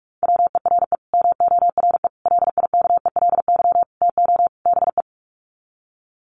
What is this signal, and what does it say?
Morse code, heard as a keyed tone, about 44 wpm, 700 Hz. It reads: WERE MORE LIKELY TO BE
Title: Sets of 5 Words Encore2 45wpm